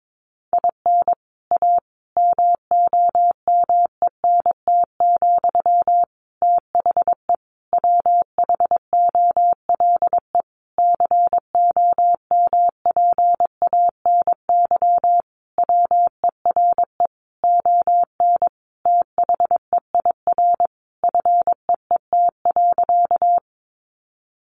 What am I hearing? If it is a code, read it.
Morse code, heard as a keyed tone, about 22 wpm, 700 Hz. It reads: IN A MOMENT, THE WHOLE COMPANY WERE ON THEIR FEET.
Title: Sentences from Top 1000 Words 22wpm